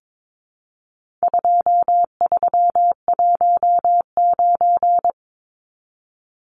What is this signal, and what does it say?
Morse code, heard as a keyed tone, about 22 wpm, 700 Hz. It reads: 2319
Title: Numbers - 4-Digit 22wpm